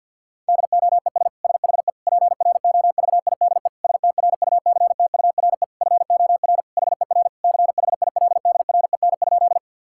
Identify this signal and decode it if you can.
Morse code, heard as a keyed tone, about 52 wpm, 700 Hz. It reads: DOES SHE PROVIDE STRUCTURE FOR HER CHILDREN?
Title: Sentences from Top 1000 Words - Review Part 6 - 50wpm